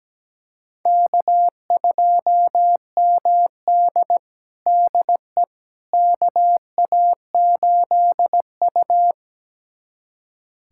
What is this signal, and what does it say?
Morse code, heard as a keyed tone, about 17 wpm, 700 Hz. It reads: K2MD DE KA8U
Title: USA Callsign DE Callsign - 4 Characters - 17wpm